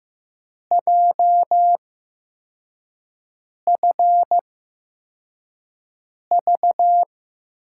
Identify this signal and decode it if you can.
Morse code, heard as a keyed tone, about 15 wpm, spaced as for 4 wpm, 700 Hz. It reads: JFV
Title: USA Call Signs - 8x Character Spacing - 15wpm